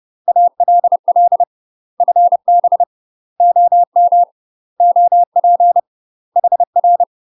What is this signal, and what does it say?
Morse code, heard as a keyed tone, about 30 wpm, 700 Hz. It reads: ALL FB OM OP HR
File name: UK MASTER QSOs TRAINING / 30WPM / UK MASTER QSOs 0051